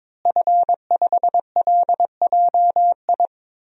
Morse code, heard as a keyed tone, about 22 wpm, 700 Hz. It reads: F5LJI
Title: DXCC-100 Call Signs - 5 Characters - Repeated 3x - 22wpm